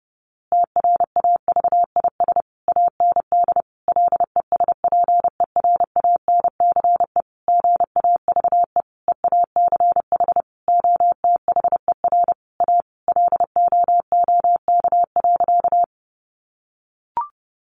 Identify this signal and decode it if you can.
Morse code, heard as a keyed tone, about 30 wpm, 700 Hz. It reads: TRAVIS AND LESPERANCE GAVE EACH OTHER A LOOK.
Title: A Sound Of Thunder 30wpm